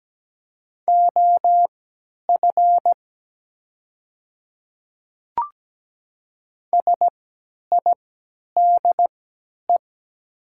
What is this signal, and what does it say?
Morse code, heard as a keyed tone, about 17 wpm, spaced as for 10 wpm, 700 Hz. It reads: OF SIDE
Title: 3x Character Spacing - Top 100 Words - 17wpm